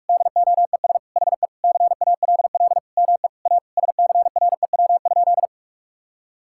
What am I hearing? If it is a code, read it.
Morse code, heard as a keyed tone, about 48 wpm, 700 Hz. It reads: DOES HE CALL ME A SCREW?
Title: Sentences from Top 1600 Words 45wpm